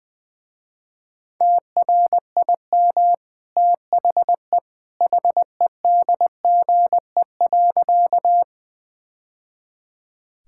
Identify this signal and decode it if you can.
Morse code, heard as a keyed tone, about 21 wpm, 700 Hz. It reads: TRIM THE HEDGE.
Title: Easy Sentences with Hints - 20wpm